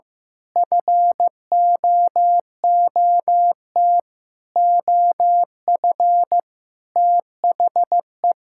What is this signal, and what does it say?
Morse code, heard as a keyed tone, about 15 wpm, 700 Hz. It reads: FOOT OF THE
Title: Sentences from Top 1000 Words - Review Part 5 - 15wpm